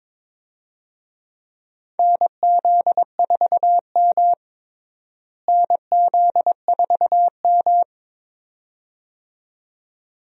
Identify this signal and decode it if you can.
Morse code, heard as a keyed tone, about 22 wpm, 700 Hz. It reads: NZ4M NZ4M
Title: USA Call Signs - 4 Characters - Repeated 2x - 22wpm